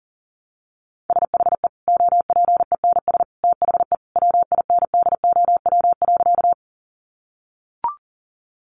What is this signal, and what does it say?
Morse code, heard as a keyed tone, about 40 wpm, 700 Hz. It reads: SHE OPENS THE WINDOW.
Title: Additional Easy Phrases 40wpm